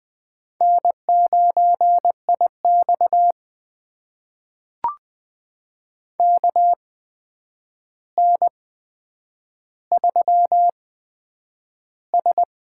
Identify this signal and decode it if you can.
Morse code, heard as a keyed tone, about 20 wpm, spaced as for 9 wpm, 700 Hz. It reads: N9IX KN3S
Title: USA Call Signs - 4 Characters - 8x Character Spacing - 20wpm